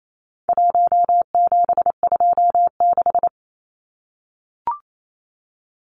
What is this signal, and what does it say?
Morse code, heard as a keyed tone, about 28 wpm, 700 Hz. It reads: 1726
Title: Numbers - 4-Digit 28wpm